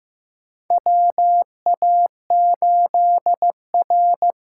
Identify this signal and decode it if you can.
Morse code, heard as a keyed tone, about 15 wpm, 700 Hz. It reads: WA8R
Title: USA Call Signs - 6 Characters - 6x Character Spacing - 15wpm